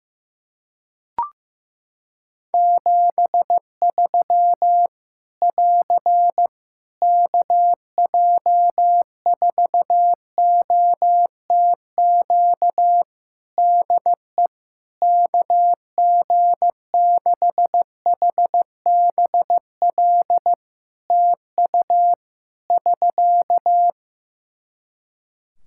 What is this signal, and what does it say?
Morse code, heard as a keyed tone, about 15 wpm, 700 Hz. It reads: 73 <AR> KJ4OTQ DE KG6HBL TU <SK>
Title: Practice QSO - Exchange 4 - Ending - 15wpm